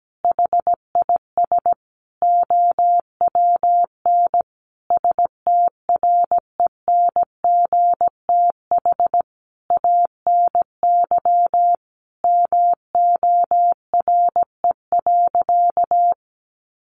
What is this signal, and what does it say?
Morse code, heard as a keyed tone, about 17 wpm, 700 Hz. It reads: HIS OWN STRENGTH ANY MORE.
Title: Sentences from Top 1300 Words 17wpm